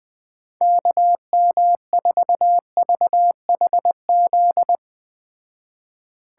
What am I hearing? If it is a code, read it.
Morse code, heard as a keyed tone, about 20 wpm, 700 Hz. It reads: KM4VHZ
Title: Speed Racing - USA 2x3 Call Signs 17wpm